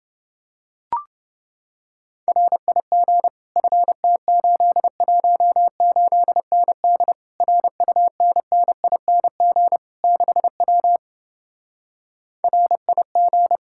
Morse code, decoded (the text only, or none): RIG FT818ND RUNNING 6W RIG